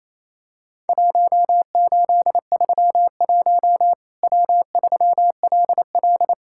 Morse code, heard as a keyed tone, about 27 wpm, 700 Hz. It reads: 1831 W3LL
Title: Worked All Europe DX Contest - QTC Exchange - 28wpm